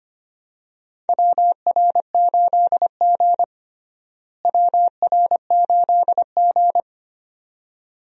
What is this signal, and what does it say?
Morse code, heard as a keyed tone, about 25 wpm, 700 Hz. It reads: WR8G WR8G
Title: USA Call Signs - 4 Characters - Repeated 2x - 25wpm